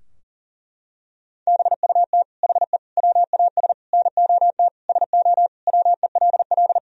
Morse code, heard as a keyed tone, about 40 wpm, 700 Hz. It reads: BUT HE WAS NOT SO WELL
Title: Sentences from Top 1000 Words - Review Part 3 - 40wpm